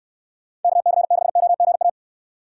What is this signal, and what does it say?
Morse code, heard as a keyed tone, about 69 wpm, 700 Hz. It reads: KC6OGA